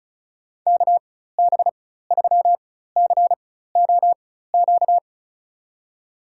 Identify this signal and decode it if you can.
Morse code, heard as a keyed tone, about 34 wpm, spaced as for 16 wpm, 700 Hz. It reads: KB3COQ
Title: USA Call Signs - 6 Characters - 4x Character Spacing - 35wpm